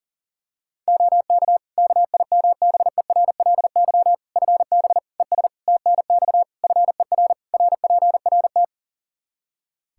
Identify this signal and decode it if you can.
Morse code, heard as a keyed tone, about 40 wpm, 700 Hz. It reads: OK KIMBERLY FB ES TNX FER RPRT
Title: Practice QSO - Exchange 2 - Info - 40wpm